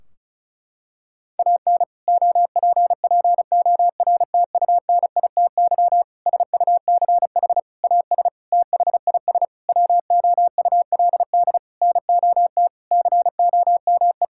Morse code, heard as a keyed tone, about 34 wpm, 700 Hz. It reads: AN OPPORTUNITY SUCH AS THIS WOULD NOT COME
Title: Sentences from Top 1600 Words 35wpm